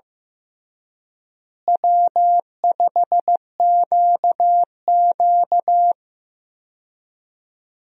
Morse code, heard as a keyed tone, about 15 wpm, 700 Hz. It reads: W5QQ
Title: USA Call Signs - 4 Characters - Repeated 3x - 15wpm